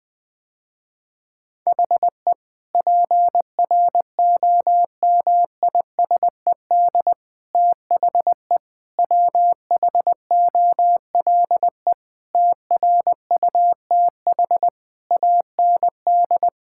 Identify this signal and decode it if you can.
Morse code, heard as a keyed tone, about 21 wpm, 700 Hz. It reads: HE PROMISED THE WHOLE TRUTH AND
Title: Sentences with 8-Word Phrases - Spoken Beginning - 20wpm